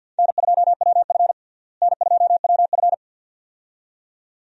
Morse code, heard as a keyed tone, about 46 wpm, 700 Hz. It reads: N2WF N2WF
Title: USA Call Signs - 4 Characters - Repeated 2x - 50wpm